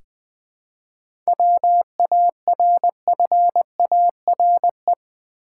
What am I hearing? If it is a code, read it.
Morse code, heard as a keyed tone, about 21 wpm, 690 Hz. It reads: WARFARE
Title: 7 Letter Words - First 3 Letters Given - 20wpm